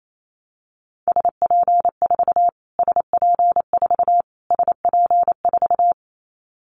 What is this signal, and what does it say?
Morse code, heard as a keyed tone, about 28 wpm, 700 Hz. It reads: SP4 SP4 SP4
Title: Callsign Prefixes for Top 25 DXCC - Repeated 3x - 28wpm